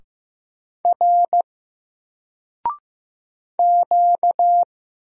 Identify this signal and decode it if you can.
Morse code, heard as a keyed tone, about 15 wpm, 700 Hz. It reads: R Q
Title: Single Character (Letter:Number:Punctuation:Prosign) Alt 15wpm